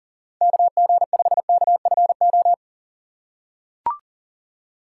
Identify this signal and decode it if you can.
Morse code, heard as a keyed tone, about 40 wpm, 700 Hz. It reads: KG5KFO
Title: USA Call Signs - 6 Characters - 2x Character Spacing - 40wpm